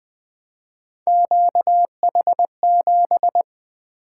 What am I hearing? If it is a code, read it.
Morse code, heard as a keyed tone, about 20 wpm, 700 Hz. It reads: QH7
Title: Three Letter-Number Combinations - 20wpm